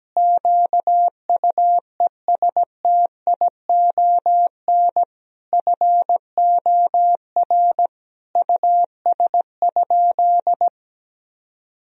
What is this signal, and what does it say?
Morse code, heard as a keyed tone, about 17 wpm, 700 Hz. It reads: QUESTION FOR US?
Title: Sentences from Top 1000 Words - Review Part 3 - 17wpm